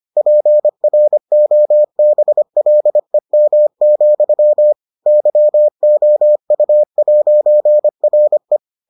Morse code, heard as a keyed tone, about 25 wpm, 600 Hz. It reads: PROBLEM, YOU'RE